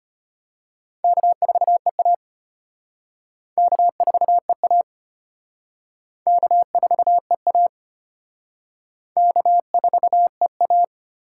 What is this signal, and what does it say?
Morse code, heard as a keyed tone, about 31 wpm, 700 Hz. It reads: K4EA K4EA K4EA K4EA
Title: Speed Racing - USA 1x2 Call Signs 25wpm